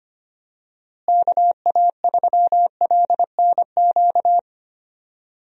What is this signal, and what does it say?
Morse code, heard as a keyed tone, about 25 wpm, 700 Hz. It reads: KA3LNQ
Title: USA Call Signs - 6 Characters - Repeated 2x - 25wpm